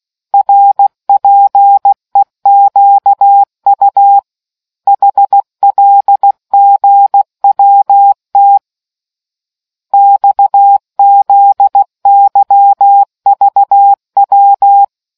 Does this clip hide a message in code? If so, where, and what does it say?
Yes: Morse code from 0.3 to 14.8 s, heard as a keyed tone, about 16 wpm, 800 Hz. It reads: RPEQU HLGWT XZYVW